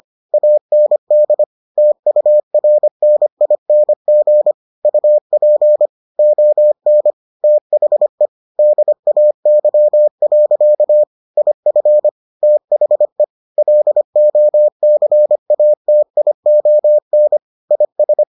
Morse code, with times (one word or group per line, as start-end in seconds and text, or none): 0.3-1.4 AND
1.8-4.5 TURNING
4.8-5.9 UP
6.2-7.1 ON
7.4-8.3 THE
8.6-11.0 DAY.
11.4-12.1 IF
12.4-13.2 THE
13.6-17.4 LOCATION
17.7-18.2 IS